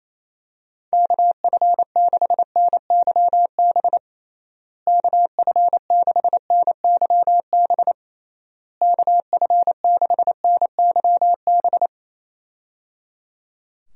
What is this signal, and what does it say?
Morse code, heard as a keyed tone, about 28 wpm, 700 Hz. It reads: KF6NYB KF6NYB KF6NYB
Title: USA Call Signs - 6 Characters - Repeated 3x - 28wpm